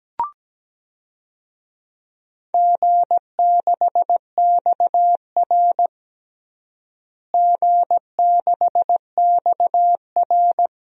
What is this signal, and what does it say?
Morse code, heard as a keyed tone, about 17 wpm, 700 Hz. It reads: G6XR G6XR
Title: DXCC-100 Call Signs - 4 Characters - Repeated 2x - 17wpm